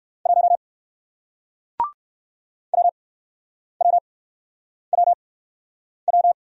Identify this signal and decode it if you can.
Morse code, heard as a keyed tone, about 61 wpm, spaced as for 9 wpm, 700 Hz. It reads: 1 WWWW